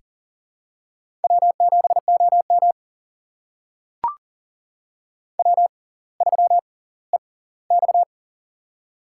Morse code, heard as a keyed tone, about 40 wpm, spaced as for 20 wpm, 700 Hz. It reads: W7OM W3EX
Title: USA Call Signs - 4 Characters - 6x Character Spacing - 40wpm